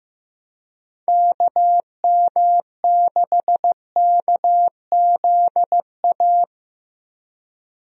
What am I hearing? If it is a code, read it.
Morse code, heard as a keyed tone, about 15 wpm, 700 Hz. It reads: KM6KZA
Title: USA Call Signs - 6 Characters - Repeated 2x - 15wpm